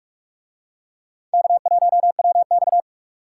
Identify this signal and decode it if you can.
Morse code, heard as a keyed tone, about 47 wpm, 700 Hz. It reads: K1WX